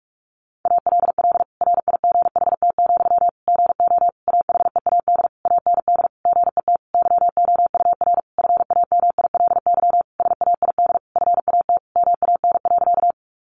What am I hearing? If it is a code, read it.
Morse code, heard as a keyed tone, about 44 wpm, 700 Hz. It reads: ALL RIGHT, GO AHEAD AND GET YOUR FAMILY SAID FAT MAN.
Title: Sentences from Top 1100 Words 45wpm